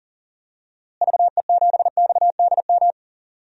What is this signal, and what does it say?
Morse code, heard as a keyed tone, about 40 wpm, 700 Hz. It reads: VE7XDM
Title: DXCC-100 Call Signs - 6 Characters - Repeated 3x - 40wpm